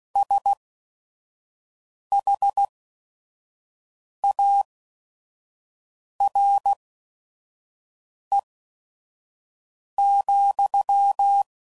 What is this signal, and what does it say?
Morse code, heard as a keyed tone, about 16 wpm, spaced as for 5 wpm, 800 Hz. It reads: SHARE,
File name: Mixed / Misc / WordPairs